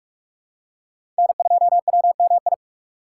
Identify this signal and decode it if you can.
Morse code, heard as a keyed tone, about 47 wpm, 700 Hz. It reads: N2WMI
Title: USA Call Signs - 4x Character Spacing - 45wpm